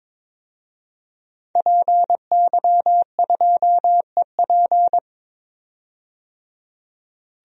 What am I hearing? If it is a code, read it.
Morse code, heard as a keyed tone, about 22 wpm, 700 Hz. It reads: PY2EP
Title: DXCC-100 Call Signs - 5 Characters - 22wpm